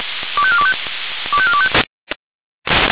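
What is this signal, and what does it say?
Five-tone selective call (ZVEI), heard as quick runs of five tones, tones of about 70 ms. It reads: 25425 25425
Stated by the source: ZVEI five-tone call received by SDR.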